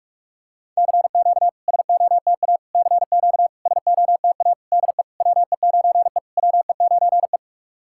Morse code, heard as a keyed tone, about 41 wpm, 690 Hz. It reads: CQ SOTA CQ SOTA DE WE9E WE9E
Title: SOTA QSO Practice 45wpm